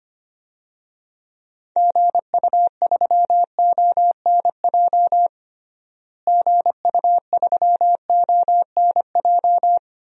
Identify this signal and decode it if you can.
Morse code, heard as a keyed tone, about 25 wpm, 700 Hz. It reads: GU3ONJ GU3ONJ
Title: DXCC-100 Call Signs - 6 Characters - Repeated 2x - 25wpm